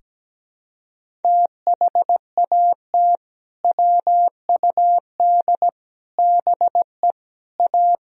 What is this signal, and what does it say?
Morse code, heard as a keyed tone, about 17 wpm, 700 Hz. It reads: THAT WUD BE A
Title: Sentences from Top 500 Words and Abbreviations 17wpm